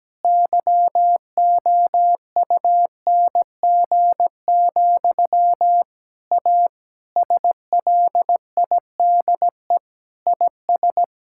Morse code, heard as a keyed tone, about 17 wpm, 700 Hz. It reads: YOUNG, A SLIDE IS
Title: Sentences from Top 1600 Words 17wpm